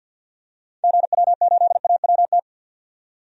Morse code, heard as a keyed tone, about 48 wpm, 700 Hz. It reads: GW8AWT
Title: DXCC-100 Call Signs - 6 Characters - 50wpm